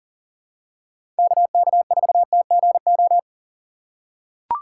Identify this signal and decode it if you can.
Morse code, heard as a keyed tone, about 40 wpm, 700 Hz. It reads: KK4TGO
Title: USA Call Signs - 6 Characters - 3x Character Spacing - 40wpm